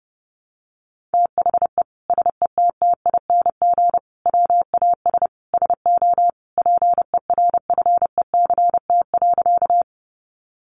Morse code, heard as a keyed tone, about 28 wpm, 700 Hz. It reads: THE SETTING WAS SO PERFECT.
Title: Sentences from Top 1300 Words 30wpm